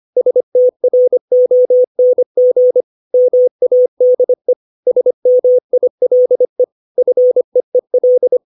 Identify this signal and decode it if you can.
Morse code, heard as a keyed tone, about 25 wpm, 500 Hz. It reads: STRONG MADE SMILE FEEL
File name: 632 words / 632 Words 25WPM